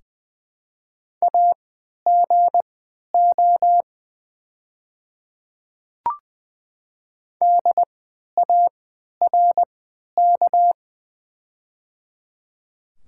Morse code, heard as a keyed tone, about 21 wpm, spaced as for 11 wpm, 700 Hz. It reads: AGO DARK
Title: 3x Character Spacing - Top 500 Words - 20wpm